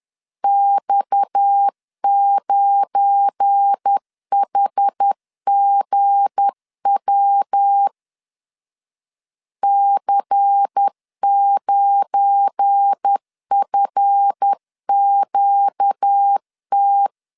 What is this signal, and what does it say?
Morse code, heard as a keyed tone, about 10 wpm, 800 Hz. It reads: X9HGW C9FQT
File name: LEZIONI / 7 GF9Z / 7 GF9 10wpm 2747